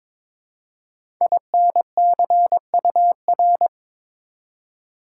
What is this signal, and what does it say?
Morse code, heard as a keyed tone, about 22 wpm, 700 Hz. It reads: INCUR
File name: 5 Letter Words - First 3 Letters Given - 22wpm